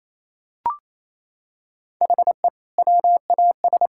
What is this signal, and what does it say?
Morse code, heard as a keyed tone, about 28 wpm, 700 Hz. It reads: HE WAS
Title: Sentences from Top 1000 Words - Review Part 5 - 28wpm